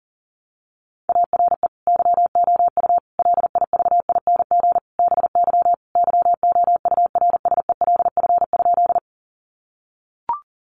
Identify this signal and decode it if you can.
Morse code, heard as a keyed tone, about 37 wpm, 700 Hz. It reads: ARE YOU LIVING BY YOURSELF?
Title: Sentences from Top 1100 Words 40wpm